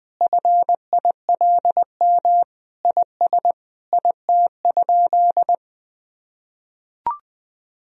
Morse code, heard as a keyed tone, about 20 wpm, 700 Hz. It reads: FILM IS IT?
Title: Sentences from Top 1000 Words - Review Part 3 - 20wpm